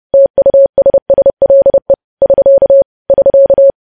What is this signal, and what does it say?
Morse code, heard as a keyed tone, about 30 wpm, 550 Hz. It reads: TUSSLE <SK> <SK>
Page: news headlines in morse code at 30 wpm